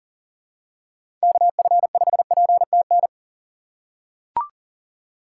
Morse code, heard as a keyed tone, about 40 wpm, 700 Hz. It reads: KF5PTN
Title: USA Call Signs - 6x Character Spacing - 40wpm